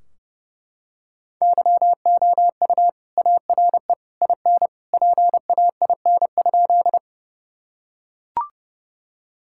Morse code, heard as a keyed tone, about 30 wpm, 700 Hz. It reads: YOU ARE IN PAIN?
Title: Sentences from Top 1600 Words 30wpm